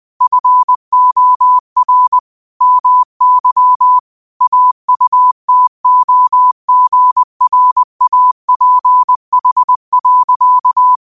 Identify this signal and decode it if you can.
Morse code, heard as a keyed tone, about 20 wpm, 1000 Hz. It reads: FOR MY AUTOGRAPH.
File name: qotdfiles / 20wpm